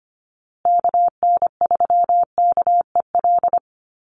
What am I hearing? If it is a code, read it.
Morse code, heard as a keyed tone, about 25 wpm, 700 Hz. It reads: KN3KEL
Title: USA Call Signs Repeated 3x 25wpm